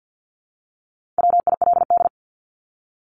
Morse code, heard as a keyed tone, about 50 wpm, 700 Hz. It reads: WILD
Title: Top 1501-1750 Words 50wpm